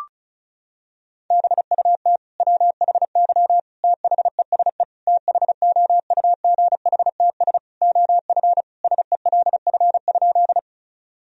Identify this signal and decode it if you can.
Morse code, heard as a keyed tone, about 36 wpm, 700 Hz. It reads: BUT WHY THESE THOUGHTS OF SELF?
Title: Sentences from Top 1000 Words - Review Part 1 - 35wpm